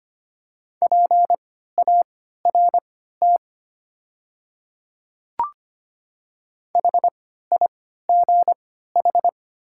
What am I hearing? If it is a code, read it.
Morse code, heard as a keyed tone, about 25 wpm, spaced as for 14 wpm, 700 Hz. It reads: PART HIGH
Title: 3x Character Spacing - Top 300 Words - 25wpm